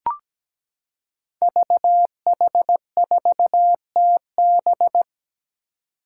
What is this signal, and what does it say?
Morse code, heard as a keyed tone, about 17 wpm, 700 Hz. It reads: VH4TB
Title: Five Challenging Letter-Number Combinations 17wpm